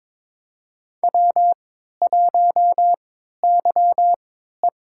Morse code, heard as a keyed tone, about 22 wpm, spaced as for 12 wpm, 700 Hz. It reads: W1YE
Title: USA Call Signs - 4 Characters - 3x Character Spacing - 22wpm